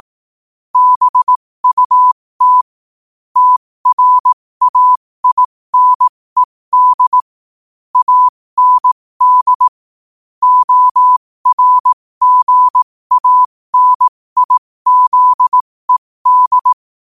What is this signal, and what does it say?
Morse code, heard as a keyed tone, about 18 wpm, spaced as for 15 wpm, 1000 Hz. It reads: BUT TRAINED AND ORGANIZED
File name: qotdfiles / 15wpm